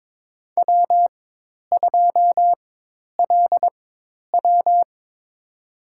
Morse code, heard as a keyed tone, about 22 wpm, spaced as for 10 wpm, 700 Hz. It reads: W2LW
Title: USA Call Signs - 4 Characters - 4x Character Spacing - 22wpm